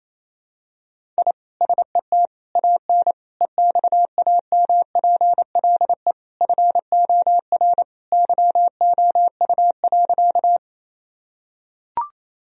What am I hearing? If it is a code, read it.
Morse code, heard as a keyed tone, about 28 wpm, 700 Hz. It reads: I SET AN EXAMPLE FOR YOU.
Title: Sentences from Top 1000 Words - Review Part 1 - 28wpm